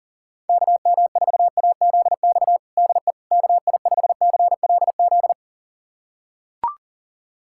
Morse code, heard as a keyed tone, about 40 wpm, 700 Hz. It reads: KM4AZX DE KI5CLZ
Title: USA Callsign DE Callsign - 6 Characters - 40wpm